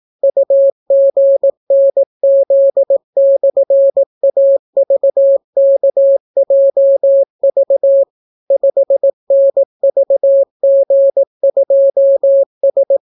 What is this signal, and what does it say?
Morse code, heard as a keyed tone, about 18 wpm, 550 Hz. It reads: UGNZ/AVKJV 5NVG2S